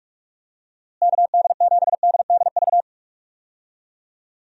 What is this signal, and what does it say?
Morse code, heard as a keyed tone, about 41 wpm, 700 Hz. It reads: KD7DDV
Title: USA Call Signs - 6 Characters - Repeated 2x - 45wpm